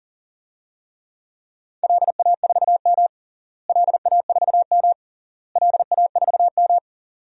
Morse code, heard as a keyed tone, about 40 wpm, 700 Hz. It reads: LA4M LA4M LA4M
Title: DXCC-100 Call Signs - 4 Characters - Repeated 3x - 40wpm